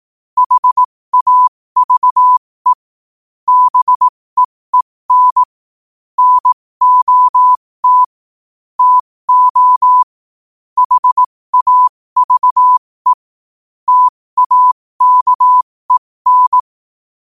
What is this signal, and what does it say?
Morse code, heard as a keyed tone, about 18 wpm, spaced as for 15 wpm, 1000 Hz. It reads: HAVE BEEN NOT TO HAVE TAKEN
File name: qotdfiles / 15wpm